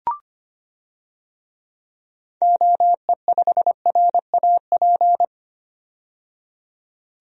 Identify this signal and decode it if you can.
Morse code, heard as a keyed tone, about 25 wpm, 700 Hz. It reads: OE5RAP
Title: DXCC-100 Call Signs - 6 Characters - 25wpm